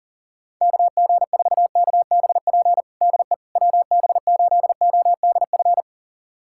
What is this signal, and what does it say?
Morse code, heard as a keyed tone, about 40 wpm, 700 Hz. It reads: KG4KBP DE WB8ODF
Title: USA Callsign DE Callsign 40wpm